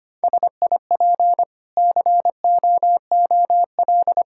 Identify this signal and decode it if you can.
Morse code, heard as a keyed tone, about 25 wpm, 690 Hz. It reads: SIP COOL
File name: Easy Sentences with Hints - 25wpm